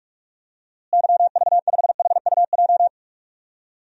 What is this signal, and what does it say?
Morse code, heard as a keyed tone, about 41 wpm, 700 Hz. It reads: YV5HUJ
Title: DXCC-100 Call Signs - 6 Characters - Repeated 2x - 45wpm